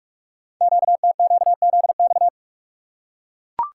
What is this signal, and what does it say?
Morse code, heard as a keyed tone, about 44 wpm, 700 Hz. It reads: QTQZX
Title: Code Groups - 5 Letters - 45wpm